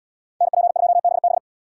Morse code, heard as a keyed tone, about 71 wpm, 700 Hz. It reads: NP2BB